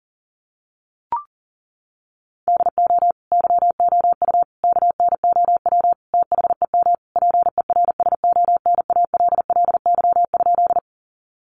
Morse code, heard as a keyed tone, about 37 wpm, 700 Hz. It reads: DO YOU KNOW THEM PERSONALLY?
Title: Sentences from Top 1400 Words 40wpm